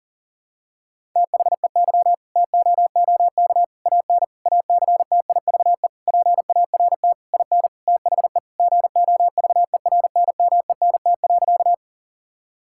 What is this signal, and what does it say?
Morse code, heard as a keyed tone, about 40 wpm, 700 Hz. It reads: THEY TOOK AN ACTIVE PART IN THE GOVERNMENT.
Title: Sentences from Top 1000 Words - Review Part 6 - 40wpm